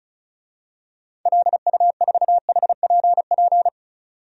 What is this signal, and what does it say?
Morse code, heard as a keyed tone, about 34 wpm, 700 Hz. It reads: LU4HPP
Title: DXCC-100 Call Signs - 6 Characters - 35wpm